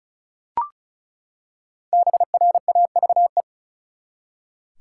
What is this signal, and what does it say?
Morse code, heard as a keyed tone, about 36 wpm, 700 Hz. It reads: BRAVE
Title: Five Letter Words - 35wpm